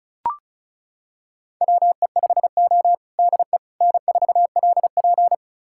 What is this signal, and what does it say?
Morse code, heard as a keyed tone, about 36 wpm, 700 Hz. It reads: WE5O DE N4LP
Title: USA Callsign DE Callsign - 4 Characters - 35wpm